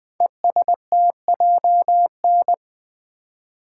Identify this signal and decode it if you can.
Morse code, heard as a keyed tone, about 20 wpm, 700 Hz. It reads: ESTJN
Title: Code Groups - 5 Letter:Numbers - 20wpm